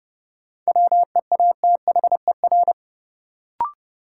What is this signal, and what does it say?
Morse code, heard as a keyed tone, about 30 wpm, 700 Hz. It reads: WEATHER
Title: Sentences from Top 800 Words - New Word Drill - 30wpm